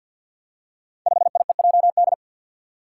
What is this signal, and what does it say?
Morse code, heard as a keyed tone, about 47 wpm, 700 Hz. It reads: 5IEJD
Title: Five Challenging Letter-Number Combinations 50wpm